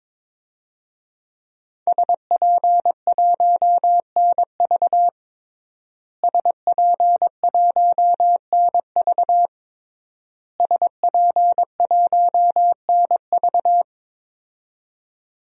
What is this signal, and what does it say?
Morse code, heard as a keyed tone, about 22 wpm, 700 Hz. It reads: SP1NV SP1NV SP1NV
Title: DXCC-100 Call Signs - 5 Characters - Repeated 3x - 22wpm